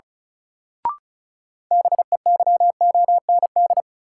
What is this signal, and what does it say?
Morse code, heard as a keyed tone, about 36 wpm, 700 Hz. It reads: BEYOND
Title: Sentences from Top 1000 Words - Review Part 6 - New Word Drill - 35wpm